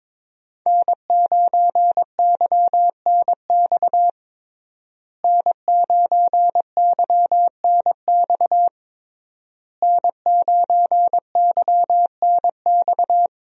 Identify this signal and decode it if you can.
Morse code, heard as a keyed tone, about 22 wpm, 700 Hz. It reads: N9YNX N9YNX N9YNX
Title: USA Call Signs - 5 Characters - Repeated 3x - 22wpm